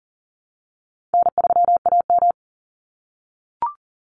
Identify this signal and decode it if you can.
Morse code, heard as a keyed tone, about 37 wpm, 700 Hz. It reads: N3AM
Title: Call Signs - 40wpm